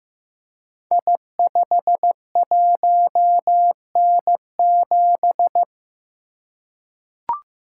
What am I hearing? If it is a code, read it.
Morse code, heard as a keyed tone, about 15 wpm, 700 Hz. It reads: I51N7
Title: Code Groups - 5 Letter:Numbers - 15wpm